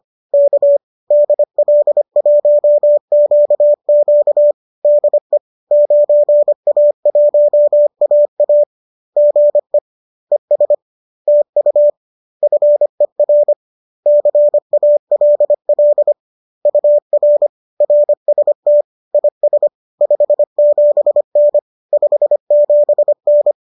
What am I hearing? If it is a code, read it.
Morse code, heard as a keyed tone, about 25 wpm, 590 Hz. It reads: K DL1QQ DE 9A1AA GE ES TU FER CALL UR RST IS 57N 57N